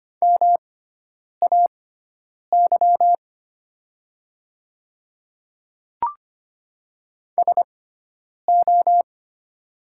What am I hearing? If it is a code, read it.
Morse code, heard as a keyed tone, about 25 wpm, spaced as for 9 wpm, 700 Hz. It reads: MAY SO
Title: 6x Character Spacing - Top 100 Words - 25wpm